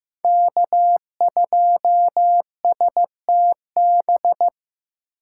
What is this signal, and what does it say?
Morse code, heard as a keyed tone, about 18 wpm, 700 Hz. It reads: K2STB
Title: Speed Racing - USA 1x3 Call Signs 15wpm